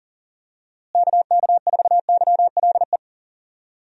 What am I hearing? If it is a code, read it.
Morse code, heard as a keyed tone, about 40 wpm, 700 Hz. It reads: KK4YLE
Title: USA Call Signs - 6 Characters - Repeated 2x - 40wpm